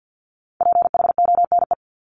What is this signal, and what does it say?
Morse code, heard as a keyed tone, about 50 wpm, 700 Hz. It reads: PHONE
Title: Multi-letter Phonemes Repeated 3x 50wpm